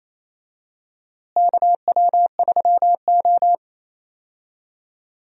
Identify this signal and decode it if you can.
Morse code, heard as a keyed tone, about 28 wpm, 700 Hz. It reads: KW3O
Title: USA Call Signs - 4 Characters - Repeated 3x - 28wpm